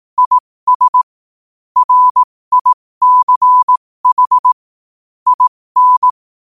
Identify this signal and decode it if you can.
Morse code, heard as a keyed tone, about 18 wpm, spaced as for 15 wpm, 1000 Hz. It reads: IS RICH IN
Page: New quotes every day in morse code at 15 Words per minute.